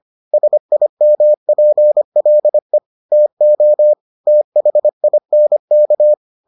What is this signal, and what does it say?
Morse code, heard as a keyed tone, about 25 wpm, 600 Hz. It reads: SIMPLE TO THINK